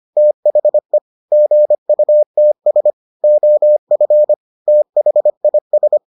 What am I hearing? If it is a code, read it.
Morse code, heard as a keyed tone, about 25 wpm, 600 Hz. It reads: THE GUTS OF THIS